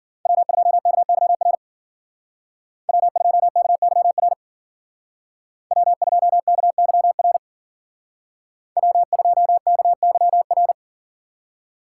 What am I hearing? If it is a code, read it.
Morse code, heard as a keyed tone, about 48 wpm, 700 Hz. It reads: W2KYR W2KYR W2KYR W2KYR
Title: Speed Racing - USA 1x3 Call Signs 40wpm